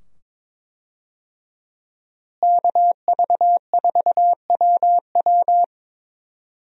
Morse code, heard as a keyed tone, about 22 wpm, 700 Hz. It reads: KV4WW
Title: USA Call Signs - 5 Characters - Repeated 2x - 22wpm